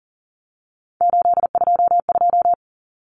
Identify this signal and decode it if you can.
Morse code, heard as a keyed tone, about 40 wpm, 700 Hz. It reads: 822
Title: Common Cut Numbers - 3-Digit 40wpm